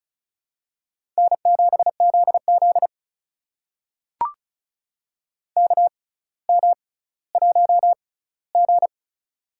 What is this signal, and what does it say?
Morse code, heard as a keyed tone, about 36 wpm, spaced as for 18 wpm, 700 Hz. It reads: N7ZZ KM1G
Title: USA Call Signs - 4 Characters - 6x Character Spacing - 35wpm